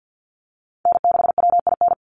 Morse code, heard as a keyed tone, about 50 wpm, 700 Hz. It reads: N6WIN
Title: Call Signs Repeated 2x - 50wpm